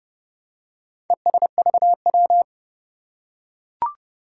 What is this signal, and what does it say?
Morse code, heard as a keyed tone, about 29 wpm, 700 Hz. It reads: ESVW
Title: Four Challenging Letter-Number Combinations 30wpm